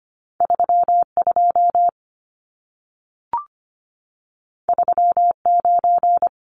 Morse code, heard as a keyed tone, about 25 wpm, 700 Hz. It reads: 32 39
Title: Numbers - 2-Digit 25wpm